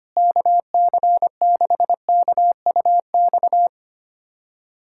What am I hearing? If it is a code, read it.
Morse code, heard as a keyed tone, about 25 wpm, 700 Hz. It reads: KC6KUX
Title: USA Callsign DE Callsign - 6 Characters - 25wpm